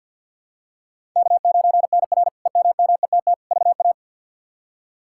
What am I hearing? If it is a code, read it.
Morse code, heard as a keyed tone, about 50 wpm, 700 Hz. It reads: K9NR EMMETT VA
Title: K1USN SST QSO Practice - 50wpm